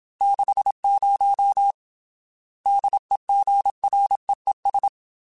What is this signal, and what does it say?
Morse code, heard as a keyed tone, about 27 wpm, 800 Hz. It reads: 60 DEGREES
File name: Weather Sets / Weather Sets-mp3s / Weather 5 22 @25